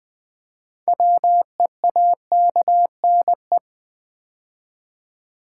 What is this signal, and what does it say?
Morse code, heard as a keyed tone, about 20 wpm, 700 Hz. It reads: WEAKNE
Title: Build A Word - 2 Letters At A Time - 8-Letter Words - 20wpm